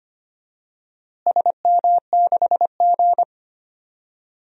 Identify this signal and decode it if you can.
Morse code, heard as a keyed tone, about 25 wpm, 700 Hz. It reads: SM6G
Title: DXCC-100 Call Signs - 4 Characters - Repeated 3x - 25wpm